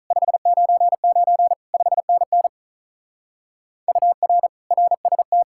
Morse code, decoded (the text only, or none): 599 5NN UR RST